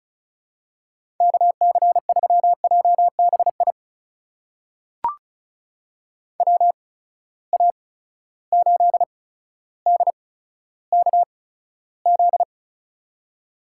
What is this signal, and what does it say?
Morse code, heard as a keyed tone, about 34 wpm, spaced as for 15 wpm, 700 Hz. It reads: KC3JBI WA8DKZ
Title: USA Call Signs - 6 Characters - 8x Character Spacing - 35wpm